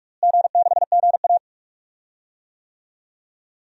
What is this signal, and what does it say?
Morse code, heard as a keyed tone, about 46 wpm, 700 Hz. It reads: G6GA
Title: DXCC-100 Call Signs - 4 Characters - Repeated 2x - 45wpm